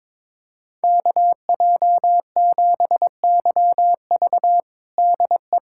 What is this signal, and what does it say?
Morse code, heard as a keyed tone, about 22 wpm, 700 Hz. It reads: KJ7YV DE
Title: USA Callsign DE Callsign - 5 Characters - 22wpm